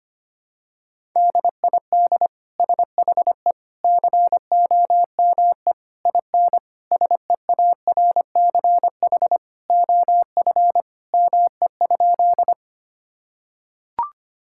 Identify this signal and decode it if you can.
Morse code, heard as a keyed tone, about 25 wpm, 700 Hz. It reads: DID SHE COME IN SEARCH OF ME?
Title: Sentences from Top 1000 Words - Review Part 5 - 25wpm